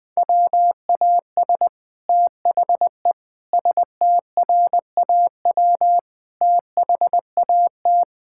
Morse code, heard as a keyed tone, about 20 wpm, 700 Hz. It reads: WAS THE STRAW THAT
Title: Sets of 8 Words - 20wpm